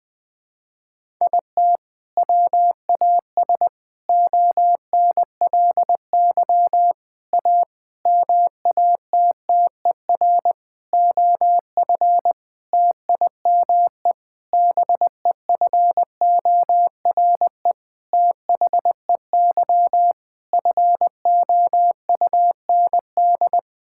Morse code, heard as a keyed tone, about 20 wpm, 700 Hz. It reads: IT WAS ONLY A MATTER OF TIME BEFORE THEY FOUND
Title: Sentences with 8-Word Phrases - Spoken Beginning - 20wpm